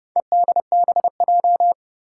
Morse code, heard as a keyed tone, about 32 wpm, 700 Hz. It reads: EDBJ
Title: Four Challenging Letter-Number Combinations 30wpm